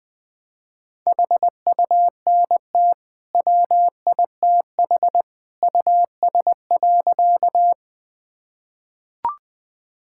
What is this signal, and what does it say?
Morse code, heard as a keyed tone, about 21 wpm, 700 Hz. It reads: HUNT WITH US.
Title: Sentences from Top 1600 Words 20wpm